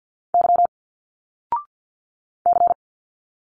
Morse code, heard as a keyed tone, about 50 wpm, 700 Hz. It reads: Y C
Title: Single Letters 50wpm